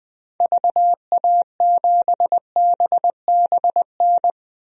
Morse code, heard as a keyed tone, about 20 wpm, 700 Hz. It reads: VA7BBN
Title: DXCC-100 Call Signs - 6 Characters - Repeated 3x - 20wpm